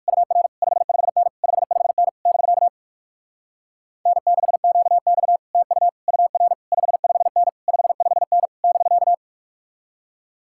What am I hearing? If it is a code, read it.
Morse code, heard as a keyed tone, about 59 wpm, 700 Hz. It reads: UR 55N 55N <BK> N6QX TU UR 55N 55N <BK>